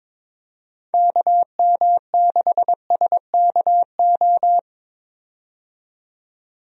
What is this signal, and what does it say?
Morse code, heard as a keyed tone, about 22 wpm, 700 Hz. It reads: KM6SKO
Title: USA Call Signs - 6 Characters - 22wpm